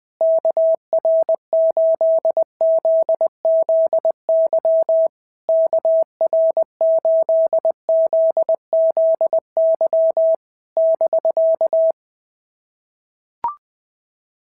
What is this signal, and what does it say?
Morse code, heard as a keyed tone, about 20 wpm, 690 Hz. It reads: KR8ZZY KR8ZZY <BK>
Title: SOTA QSO Practice 20wpm